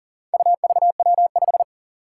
Morse code, heard as a keyed tone, about 40 wpm, 700 Hz. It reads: UVW5
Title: Four Challenging Letter-Number Combinations 40wpm